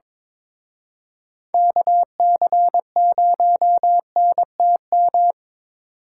Morse code, heard as a keyed tone, about 26 wpm, 700 Hz. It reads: KC0NTM
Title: Speed Racing - USA 2x3 Call Signs 22wpm